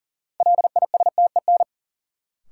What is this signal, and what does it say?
Morse code, heard as a keyed tone, about 40 wpm, 700 Hz. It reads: LISTEN
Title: Sentences from Top 400 Words - New Word Drill - 40wpm